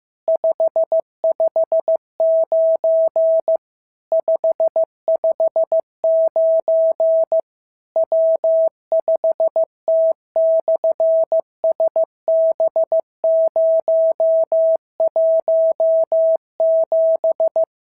Morse code, heard as a keyed tone, about 15 wpm, 650 Hz. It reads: 559 559 W5T/SB017
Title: SOTA QSO Practice 15wpm